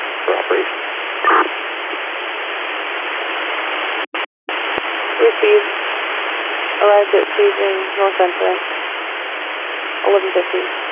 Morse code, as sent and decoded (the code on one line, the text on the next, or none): none